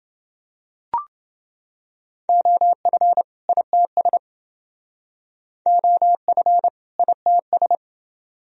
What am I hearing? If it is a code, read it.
Morse code, heard as a keyed tone, about 28 wpm, 700 Hz. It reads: OF ITS OF ITS